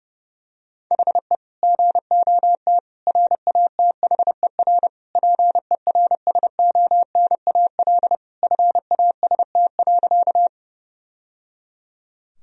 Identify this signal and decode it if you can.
Morse code, heard as a keyed tone, about 30 wpm, 700 Hz. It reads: HE GOT RATHER PERSONAL FAST.
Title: Sentences from Top 1000 Words - Review Part 4 - 30wpm